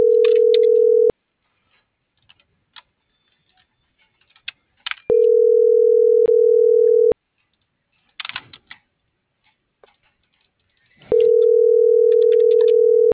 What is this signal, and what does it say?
Ringback tone (North America), 440 + 480 Hz.
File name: CENTRAL MABE / out xquenda.trinidad - .74908